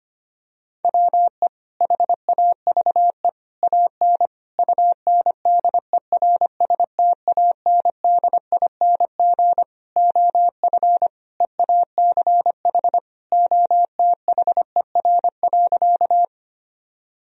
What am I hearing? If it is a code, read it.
Morse code, heard as a keyed tone, about 25 wpm, 700 Hz. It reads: WE HAVE AN UNDERSTANDING OF EACH OTHER.
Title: Sentences from Top 1000 Words - Review Part 3 - 25wpm